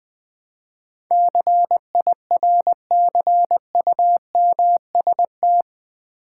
Morse code, heard as a keyed tone, about 20 wpm, 700 Hz. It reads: CIRCUMST
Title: Build A Word - 4 Letters At A Time - 12-Letter Words - 20wpm